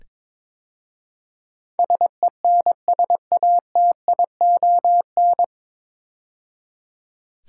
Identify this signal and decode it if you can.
Morse code, heard as a keyed tone, about 22 wpm, 700 Hz. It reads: SENSATION
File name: 9 Letter Words - First 3 Letters Given - 22wpm